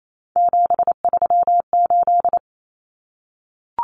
Morse code, heard as a keyed tone, about 28 wpm, 700 Hz. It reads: 738
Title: Common Cut Numbers - 3-Digit 28wpm